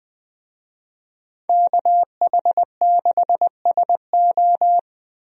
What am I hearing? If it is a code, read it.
Morse code, heard as a keyed tone, about 20 wpm, 700 Hz. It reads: KH6SO
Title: DXCC-100 Call Signs - 5 Characters - 20wpm